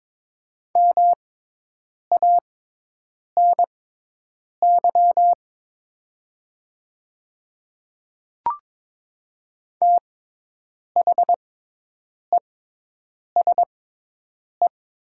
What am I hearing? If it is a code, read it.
Morse code, heard as a keyed tone, about 22 wpm, spaced as for 8 wpm, 700 Hz. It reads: MANY THESE
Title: 6x Character Spacing - Top 100 Words - 22wpm